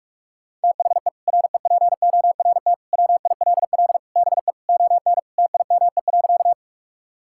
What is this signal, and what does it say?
Morse code, heard as a keyed tone, about 46 wpm, 700 Hz. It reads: THE REPORT WILL BE ON TIME.
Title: Sentences from Top 1000 Words - Review Part 6 - 45wpm